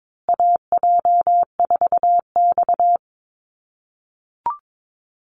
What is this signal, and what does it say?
Morse code, heard as a keyed tone, about 22 wpm, 700 Hz. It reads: AJ4X
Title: USA Call Signs Repeated 2x 22wpm